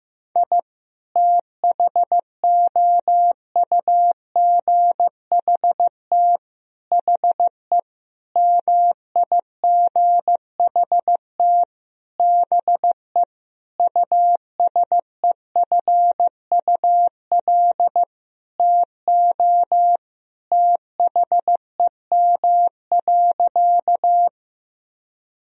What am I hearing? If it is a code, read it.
Morse code, heard as a keyed tone, about 15 wpm, 700 Hz. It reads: I THOUGHT HE MIGHT BE USEFUL TO THEM.
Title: Sentences from Top 1000 Words - Review Part 4 - 15wpm